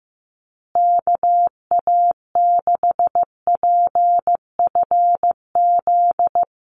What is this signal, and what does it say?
Morse code, heard as a keyed tone, about 15 wpm, 700 Hz. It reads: KA6PFZ
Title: USA Call Signs Repeated 2x 15wpm